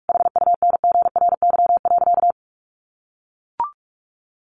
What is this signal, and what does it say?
Morse code, heard as a keyed tone, about 44 wpm, 700 Hz. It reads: HUNGRY.
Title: Sentences from Top 1400 Words 45wpm